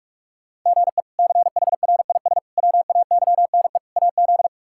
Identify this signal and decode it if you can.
Morse code, heard as a keyed tone, about 46 wpm, 700 Hz. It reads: GE CHRIS WAYNE AZ
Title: K1USN SST QSO Practice - 45wpm